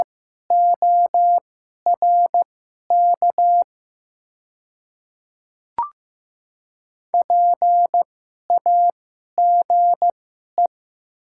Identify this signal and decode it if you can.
Morse code, heard as a keyed tone, about 15 wpm, spaced as for 11 wpm, 700 Hz. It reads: ORK PAGE
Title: 2x Character Spacing - Top 300 Words - 15wpm